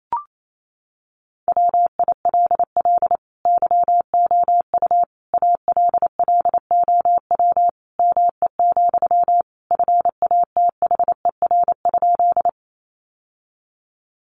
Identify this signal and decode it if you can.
Morse code, heard as a keyed tone, about 28 wpm, 700 Hz. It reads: WILL YOU ALLOW ME, FATHER?
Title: Sentences from Top 700 Words 28wpm